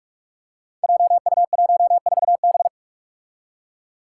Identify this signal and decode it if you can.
Morse code, heard as a keyed tone, about 44 wpm, 700 Hz. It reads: JU14B
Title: Five Challenging Letter-Number Combinations 45wpm